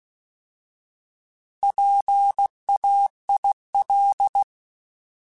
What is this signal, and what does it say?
Morse code, heard as a keyed tone, about 16 wpm, 800 Hz. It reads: PAIL
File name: Alphabetical / Alphabetical mp3s / Ps @15